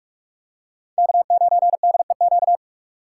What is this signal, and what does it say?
Morse code, heard as a keyed tone, about 46 wpm, 700 Hz. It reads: K9DEQ
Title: USA Call Signs - 5 Characters - 2x Character Spacing - 45wpm